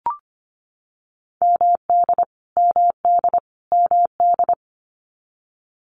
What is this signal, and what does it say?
Morse code, heard as a keyed tone, about 25 wpm, 700 Hz. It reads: MD MD MD
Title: US States Repeated 3x 25wpm